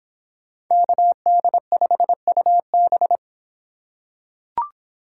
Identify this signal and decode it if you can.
Morse code, heard as a keyed tone, about 26 wpm, 700 Hz. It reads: KD5UB